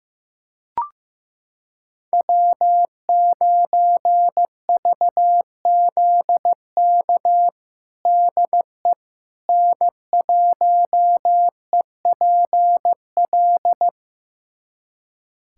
Morse code, heard as a keyed tone, about 15 wpm, 700 Hz. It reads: W9VZK DE N1EPL
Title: USA Callsign DE Callsign - 5 Characters - 15wpm